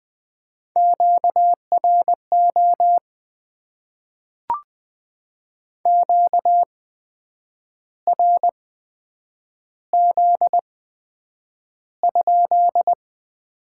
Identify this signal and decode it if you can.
Morse code, heard as a keyed tone, about 20 wpm, spaced as for 8 wpm, 700 Hz. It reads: QRO QRZ?
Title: Top 90 QSO Elements - 8x Character Spacing - 20wpm